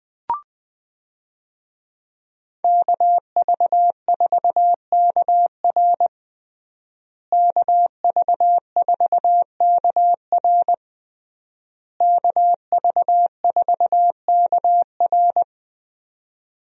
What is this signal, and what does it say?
Morse code, heard as a keyed tone, about 20 wpm, 700 Hz. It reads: KV4KR KV4KR KV4KR
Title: USA Call Signs - 5 Characters - Repeated 3x - 20wpm